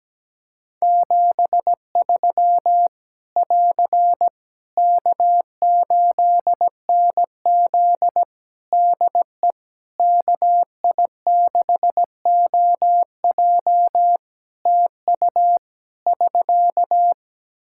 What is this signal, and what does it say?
Morse code, heard as a keyed tone, about 17 wpm, 700 Hz. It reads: 73 <AR> K8NZ DE KI6OJ TU <SK>
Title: Practice QSO - Exchange 4 - Ending - 17wpm